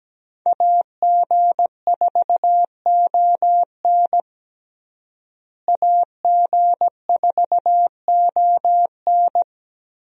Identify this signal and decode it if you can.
Morse code, heard as a keyed tone, about 17 wpm, 700 Hz. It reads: AG4ON AG4ON
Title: USA Call Signs - 5 Characters - Repeated 2x - 17wpm